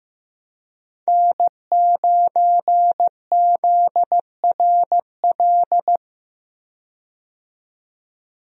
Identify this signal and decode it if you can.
Morse code, heard as a keyed tone, about 15 wpm, 700 Hz. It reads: N9ZRL
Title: USA Call Signs - 5 Characters - 15wpm